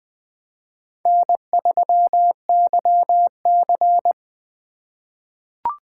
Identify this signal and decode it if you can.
Morse code, heard as a keyed tone, about 20 wpm, 700 Hz. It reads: N3YC
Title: USA Call Signs - 4 Characters - 2x Character Spacing - 20wpm